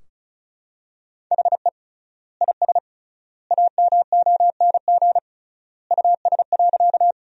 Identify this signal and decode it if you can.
Morse code, heard as a keyed tone, about 36 wpm, 700 Hz. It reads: HE IS AMONG US.
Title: 3x Word-Spacing - Sentences from Top 500 Words - 35wpm